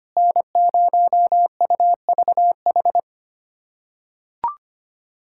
Morse code, heard as a keyed tone, about 25 wpm, 700 Hz. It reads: N0UVH
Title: USA Callsign DE Callsign - 5 Characters - 25wpm